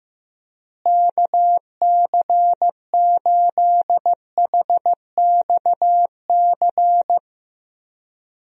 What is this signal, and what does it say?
Morse code, heard as a keyed tone, about 16 wpm, 700 Hz. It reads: KC8HXC
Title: USA Call Signs - 6 Characters - Repeated 2x - 15wpm